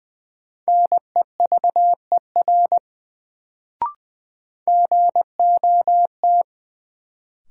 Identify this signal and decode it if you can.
Morse code, heard as a keyed tone, about 20 wpm, 700 Hz. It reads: NEVER GOT
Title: Sentences from Top 300 Words - New Word Drill - 20wpm